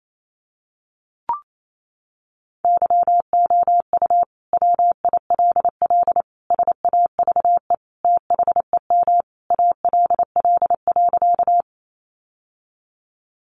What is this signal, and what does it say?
Morse code, heard as a keyed tone, about 28 wpm, 700 Hz. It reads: YOU WILL SAVE THEM ALL.
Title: Sentences from Top 1100 Words 28wpm